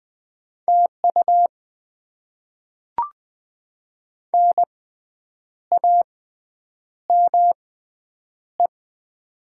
Morse code, heard as a keyed tone, about 20 wpm, spaced as for 8 wpm, 700 Hz. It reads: TU NAME
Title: Top 20 QSO Elements - 6x Character Spacing - 20wpm